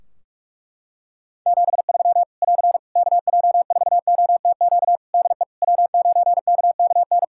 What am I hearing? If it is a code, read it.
Morse code, heard as a keyed tone, about 41 wpm, 700 Hz. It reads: 73 <AR> KJ4OTQ DE W9KKN
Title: Practice QSO - Exchange 4 - Ending - 45wpm